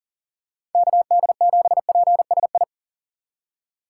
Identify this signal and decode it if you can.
Morse code, heard as a keyed tone, about 40 wpm, 700 Hz. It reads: KD7PSI
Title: USA Call Signs - 6 Characters - 6x Character Spacing - 40wpm